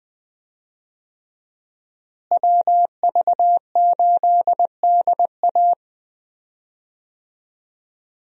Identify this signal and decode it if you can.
Morse code, heard as a keyed tone, about 20 wpm, 700 Hz. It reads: WV8DA
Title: USA Call Signs - 5 Characters - Repeated 3x - 20wpm